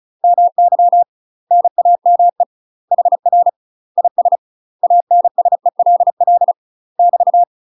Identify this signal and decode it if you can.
Morse code, heard as a keyed tone, about 36 wpm, 700 Hz. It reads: MY NAME HR IS ANSELL <BT>
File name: UK MASTER QSOs TRAINING / 35WPM / UK MASTER QSOs 0012